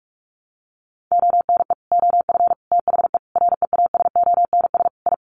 Morse code, heard as a keyed tone, about 44 wpm, 700 Hz. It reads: ONE OF THE REASONS I
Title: Sets of 5 Words Encore2 45wpm